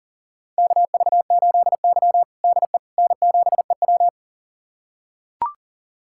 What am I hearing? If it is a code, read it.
Morse code, heard as a keyed tone, about 40 wpm, 700 Hz. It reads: KV8Y DE N7EW
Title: USA Callsign DE Callsign - 4 Characters - 40wpm